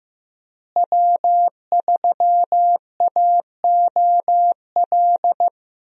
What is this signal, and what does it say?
Morse code, heard as a keyed tone, about 15 wpm, 700 Hz. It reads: W3AOL
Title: USA Call Signs - 5 Characters - Repeated 2x - 15wpm